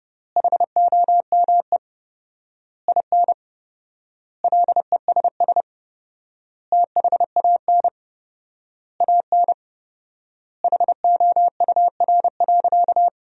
Morse code, heard as a keyed tone, about 32 wpm, 700 Hz. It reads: HOME IN LESS THAN AN HOUR.
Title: 4x Word-Spacing - Sentences from Top 500 Words - 30wpm